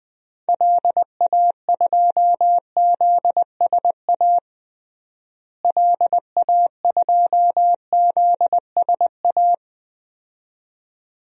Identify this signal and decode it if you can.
Morse code, heard as a keyed tone, about 20 wpm, 700 Hz. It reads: LA2ZSA LA2ZSA
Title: DXCC-100 Call Signs - 6 Characters - Repeated 2x - 20wpm